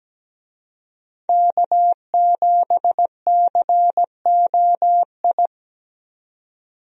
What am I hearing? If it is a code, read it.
Morse code, heard as a keyed tone, about 17 wpm, 700 Hz. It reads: K7COI
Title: USA Call Signs - 5 Characters - 17wpm